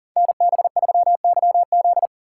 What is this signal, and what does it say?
Morse code, heard as a keyed tone, about 41 wpm, 700 Hz. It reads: NB3YZ
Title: Code Groups - 5 Letter:Numbers - 40wpm